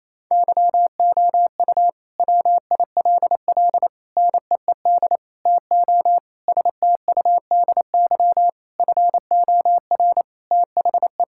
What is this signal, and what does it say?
Morse code, heard as a keyed tone, about 28 wpm, 700 Hz. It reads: YOU WILL NEED TO STUDY FOR THE
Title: Sentences from Top 1000 Words - Review Part 5 - 28wpm